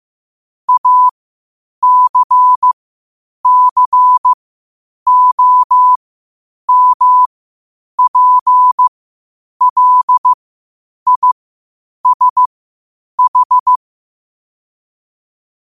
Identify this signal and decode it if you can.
Morse code, heard as a keyed tone, about 15 wpm, spaced as for 8 wpm, 1000 Hz. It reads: ACCOMPLISH
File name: qotdfiles / 8wpm